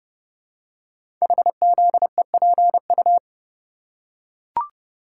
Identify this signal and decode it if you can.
Morse code, heard as a keyed tone, about 29 wpm, 700 Hz. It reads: HZEPU
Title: Code Groups - 5 Letters - 30wpm